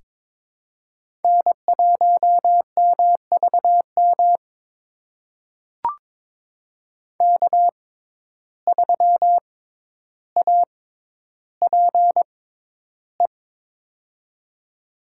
Morse code, heard as a keyed tone, about 21 wpm, spaced as for 11 wpm, 700 Hz. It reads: N1MVM K3APE
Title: USA Call Signs - 5 Characters - 6x Character Spacing - 22wpm